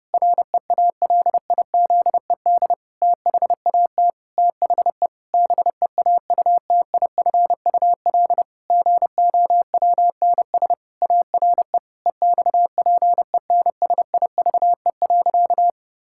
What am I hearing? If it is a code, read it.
Morse code, heard as a keyed tone, about 32 wpm, 700 Hz. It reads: REALIZED THAT THE BEAUTIFUL GOWNS ARE EXPENSIVE.
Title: Pangram Challenge v2 - 30wpm